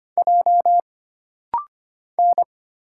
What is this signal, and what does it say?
Morse code, heard as a keyed tone, about 25 wpm, 700 Hz. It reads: J N
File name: Single Character (Letter:Number:Punctuation:Prosign) - Rapid-Fire - 25wpm